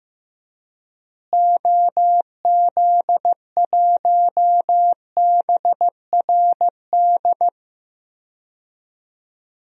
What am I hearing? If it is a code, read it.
Morse code, heard as a keyed tone, about 15 wpm, 700 Hz. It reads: OZ1BRD
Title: DXCC-100 Call Signs - 6 Characters - Repeated 3x - 15wpm